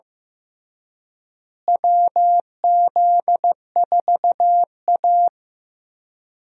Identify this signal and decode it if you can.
Morse code, heard as a keyed tone, about 15 wpm, 700 Hz. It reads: WZ4A
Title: USA Call Signs - 4 Characters - Repeated 3x - 15wpm